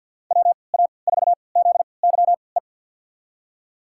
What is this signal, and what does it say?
Morse code, heard as a keyed tone, about 52 wpm, spaced as for 29 wpm, 700 Hz. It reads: WA4ZYE
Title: USA Call Signs - 6 Characters - 3x Character Spacing - 50wpm